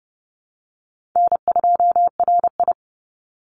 Morse code, heard as a keyed tone, about 30 wpm, 700 Hz. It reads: N2RI
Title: Call Signs - 30wpm